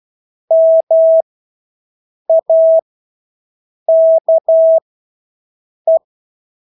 Morse code, heard as a keyed tone, about 12 wpm, spaced as for 6 wpm, 650 Hz. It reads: MAKE
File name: Engelske ord 12-6 wpm